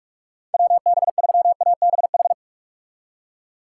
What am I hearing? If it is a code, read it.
Morse code, heard as a keyed tone, about 46 wpm, 700 Hz. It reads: WB3ABH
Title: USA Call Signs - 6 Characters - 4x Character Spacing - 45wpm